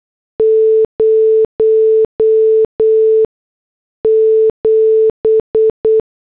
Morse code, heard as a keyed tone, about 8 wpm, 440 Hz. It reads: 0 7
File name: morse / headlines